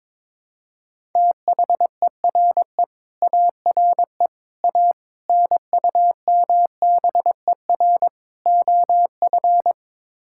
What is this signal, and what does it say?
Morse code, heard as a keyed tone, about 23 wpm, 700 Hz. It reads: THERE ARE A NUMBER OF
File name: practice / 22wpm / Build A Phrase - 5-Word Phrases - 22wpm